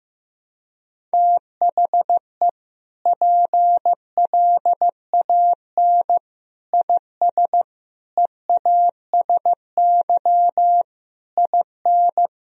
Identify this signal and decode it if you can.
Morse code, heard as a keyed tone, about 15 wpm, 700 Hz. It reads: THE PLAN IS EASY IN
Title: Sentences from Top 1000 Words - Review Part 4 - 15wpm